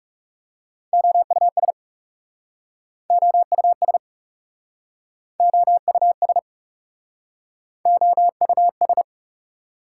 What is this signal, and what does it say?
Morse code, heard as a keyed tone, about 37 wpm, 700 Hz. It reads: OUS OUS OUS OUS
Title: Speed-Racing Prefixes and Suffixes 30wpm